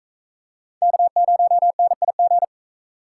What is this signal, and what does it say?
Morse code, heard as a keyed tone, about 39 wpm, 700 Hz. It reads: K0NIG